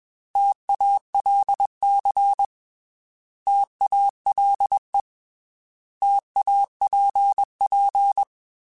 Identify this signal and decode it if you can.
Morse code, heard as a keyed tone, about 21 wpm, 800 Hz. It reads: TALC TALE TAPP
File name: T Prefixes / T mp3s